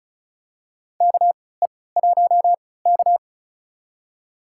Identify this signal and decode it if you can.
Morse code, heard as a keyed tone, about 36 wpm, spaced as for 20 wpm, 690 Hz. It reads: KE1K
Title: USA Call Signs - 4 Characters - 3x Character Spacing - 35wpm